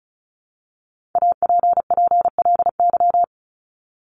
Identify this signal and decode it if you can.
Morse code, heard as a keyed tone, about 34 wpm, 700 Hz. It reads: APPLY
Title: ED Suffix - Top 100 Words with Base Words 35wpm